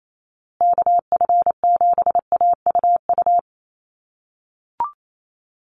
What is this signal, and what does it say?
Morse code, heard as a keyed tone, about 28 wpm, 700 Hz. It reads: KF7AUU
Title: USA Call Signs Repeated 3x 28wpm